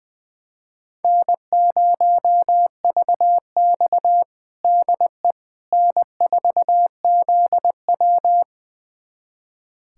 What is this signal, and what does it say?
Morse code, heard as a keyed tone, about 20 wpm, 700 Hz. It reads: N0VX DE N4ZW
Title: USA Callsign DE Callsign - 4 Characters - 20wpm